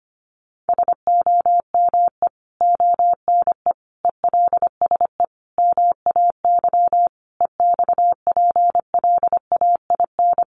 Morse code, heard as a keyed tone, about 25 wpm, 700 Hz. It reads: SOME ONE ELSE MAY EXPLAIN
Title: Sentences from Top 1100 Words 25wpm